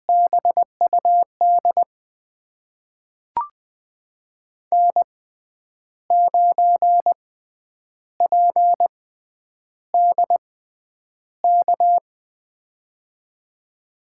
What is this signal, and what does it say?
Morse code, heard as a keyed tone, about 20 wpm, spaced as for 9 wpm, 700 Hz. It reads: BUD N9PDK
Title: USA Call Signs - 6x Character Spacing - 20wpm